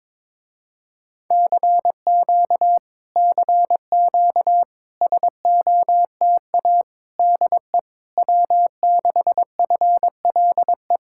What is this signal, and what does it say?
Morse code, heard as a keyed tone, about 22 wpm, 700 Hz. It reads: CQ CQ SOTA DE W6FLE
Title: Speed-Racing Common QSO Phrases - 22wpm